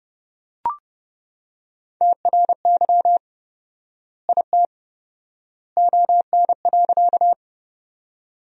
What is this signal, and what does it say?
Morse code, heard as a keyed tone, about 31 wpm, 700 Hz. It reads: TRY IT ON.
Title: 4x Word-Spacing - Sentences from Top 200 Words - 30wpm